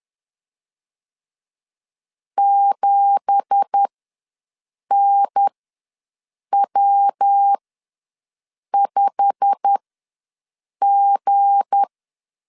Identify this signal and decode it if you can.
Morse code, heard as a keyed tone, about 10 wpm, spaced as for 6 wpm, 800 Hz. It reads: 7NW5G
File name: LEZIONI / 9 2867 ws 10wpm 4215